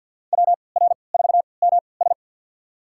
Morse code, heard as a keyed tone, about 52 wpm, spaced as for 29 wpm, 700 Hz. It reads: WR4MS
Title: USA Call Signs - 5 Characters - 3x Character Spacing - 50wpm